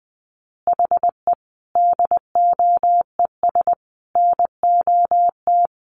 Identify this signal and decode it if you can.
Morse code, heard as a keyed tone, about 20 wpm, 700 Hz. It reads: HE DOES NOT
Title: Additional Easy Phrases 20wpm